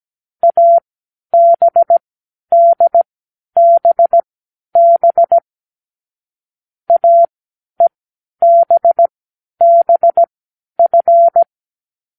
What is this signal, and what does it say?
Morse code, heard as a keyed tone, about 17 wpm, spaced as for 10 wpm, 680 Hz. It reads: ABDBB AEBBF
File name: k7qo / Lesson9(A-F)